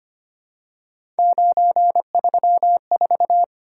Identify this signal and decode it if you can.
Morse code, heard as a keyed tone, about 25 wpm, 700 Hz. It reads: 934
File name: Three Letter-Number Combinations - 25wpm